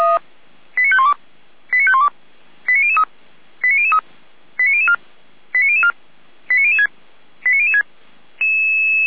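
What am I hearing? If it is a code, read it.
Five-tone selective call (ZVEI), heard as quick runs of five tones, tones of about 80 ms. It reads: 89512 89512 89003 89003 89004 89004 89006 89006
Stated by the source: ZVEI five-tone selective call received via SDR.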